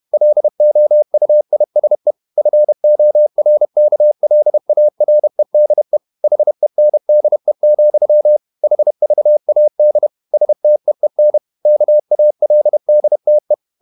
Morse code, heard as a keyed tone, about 31 wpm, 600 Hz. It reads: LOUISE FORKLAREDE HENDE, HVAD STEEN KALDTE
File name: Ferie QRQ 31wpm